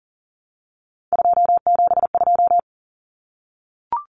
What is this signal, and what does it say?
Morse code, heard as a keyed tone, about 40 wpm, 700 Hz. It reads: 272
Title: Numbers - 3-Digit 40wpm